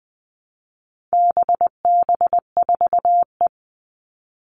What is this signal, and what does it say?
Morse code, heard as a keyed tone, about 20 wpm, 700 Hz. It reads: BB4E
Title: All Cut Numbers - 4-Digit 20wpm